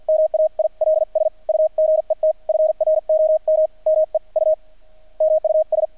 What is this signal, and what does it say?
Morse code, heard as a keyed tone, about 48 wpm, spaced as for 42 wpm, 630 Hz, with radio noise. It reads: MAILS U GET FROM ME U MUS
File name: archived / CRSnet